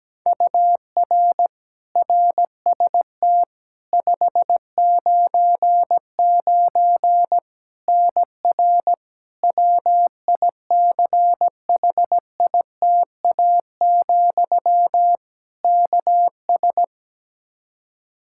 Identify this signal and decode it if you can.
Morse code, heard as a keyed tone, about 17 wpm, 700 Hz. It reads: UR RST 599 NR WICHITA, KS
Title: K1USN SST QSO Practice 17wpm